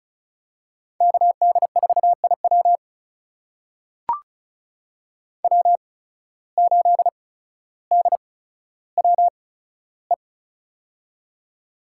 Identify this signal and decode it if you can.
Morse code, heard as a keyed tone, about 34 wpm, spaced as for 15 wpm, 700 Hz. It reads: KD4IW W8DWE
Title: USA Call Signs - 5 Characters - 8x Character Spacing - 35wpm